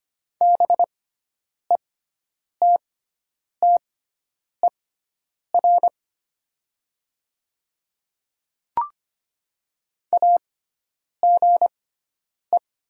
Morse code, heard as a keyed tone, about 25 wpm, spaced as for 9 wpm, 700 Hz. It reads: BETTER AGE
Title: 6x Character Spacing - Top 500 Words - 25wpm